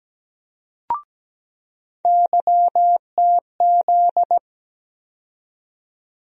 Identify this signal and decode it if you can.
Morse code, heard as a keyed tone, about 17 wpm, 700 Hz. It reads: YTZ
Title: USA Call Sign Postfixes - 3 Letters 17wpm